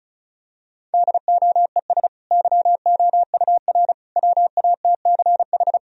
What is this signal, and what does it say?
Morse code, heard as a keyed tone, about 34 wpm, 700 Hz. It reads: DOES YOUR WATCH
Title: Sentences from Top 1000 Words - Review Part 1 - 35wpm